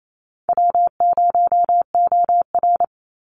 Morse code, heard as a keyed tone, about 28 wpm, 700 Hz. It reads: W0OR
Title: Easy Call Signs Repeated 3x - 28wpm